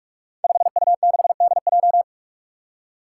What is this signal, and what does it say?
Morse code, heard as a keyed tone, about 44 wpm, 700 Hz. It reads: 5U6DJ
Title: Five Challenging Letter-Number Combinations 45wpm